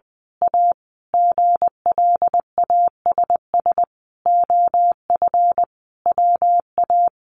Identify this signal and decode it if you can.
Morse code, heard as a keyed tone, about 20 wpm, 700 Hz. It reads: A GLASS OF WA
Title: Sentences from Top 1300 Words 20wpm